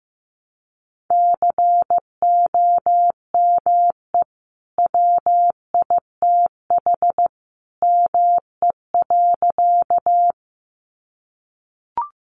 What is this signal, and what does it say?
Morse code, heard as a keyed tone, about 15 wpm, 700 Hz. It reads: COME WITH ME.
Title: Sentences from Top 200 Words 15wpm